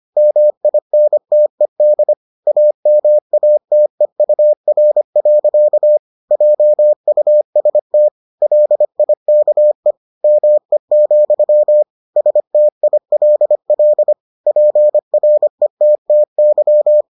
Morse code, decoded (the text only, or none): MINTED AMATEUR. JUST LIKE ME, STILL PRETTY